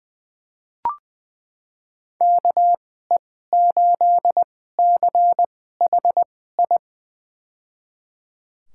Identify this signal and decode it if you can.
Morse code, heard as a keyed tone, about 21 wpm, spaced as for 15 wpm, 700 Hz. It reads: KE8CHI
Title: USA Call Signs - 6 Characters - 2x Character Spacing - 20wpm